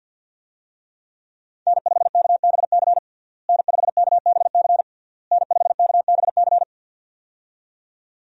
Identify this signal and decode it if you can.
Morse code, heard as a keyed tone, about 48 wpm, 700 Hz. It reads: N5KBC N5KBC N5KBC
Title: USA Call Signs - 5 Characters - Repeated 3x - 50wpm